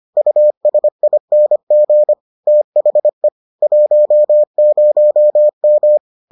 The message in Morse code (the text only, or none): USING THE 10M